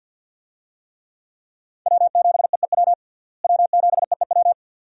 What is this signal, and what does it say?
Morse code, heard as a keyed tone, about 47 wpm, 700 Hz. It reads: W7EEW W7EEW
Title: USA Call Signs - 5 Characters - Repeated 2x - 50wpm